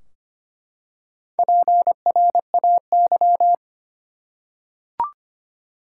Morse code, heard as a keyed tone, about 25 wpm, 700 Hz. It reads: PRAY
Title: Four Letter Words - 25wpm